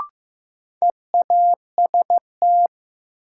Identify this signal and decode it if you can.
Morse code, heard as a keyed tone, about 15 wpm, 700 Hz. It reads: EAST
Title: Sentences from Top 1200 Words - New Word Drill Alt - 15wpm